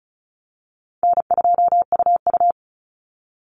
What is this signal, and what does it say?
Morse code, heard as a keyed tone, about 34 wpm, 700 Hz. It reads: N2UU
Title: Call Signs - 35wpm